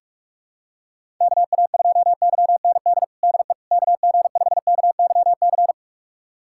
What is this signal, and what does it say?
Morse code, heard as a keyed tone, about 43 wpm, 700 Hz. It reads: KA2YND DE KG5KYC
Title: USA Callsign DE Callsign - 6 Characters - 45wpm